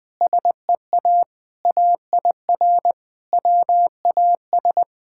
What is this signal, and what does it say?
Morse code, heard as a keyed tone, about 21 wpm, 700 Hz. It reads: SEA AIR WAS
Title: Sentences from Top 1600 Words 20wpm